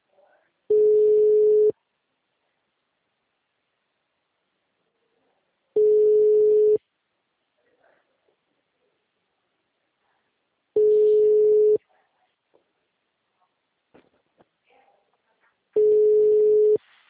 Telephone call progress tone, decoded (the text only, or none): ringback tone